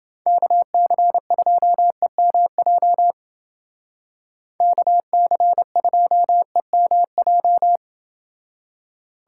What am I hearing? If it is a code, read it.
Morse code, heard as a keyed tone, about 29 wpm, 700 Hz. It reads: KC2EMJ KC2EMJ